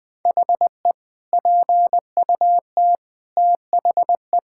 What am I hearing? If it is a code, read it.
Morse code, heard as a keyed tone, about 20 wpm, 700 Hz. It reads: HE PUT THE
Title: Sentences with 8-Word Phrases - 20wpm